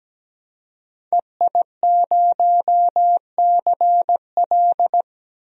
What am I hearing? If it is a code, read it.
Morse code, heard as a keyed tone, about 17 wpm, 700 Hz. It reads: EI0CL
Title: DXCC-100 Call Signs - 5 Characters - 17wpm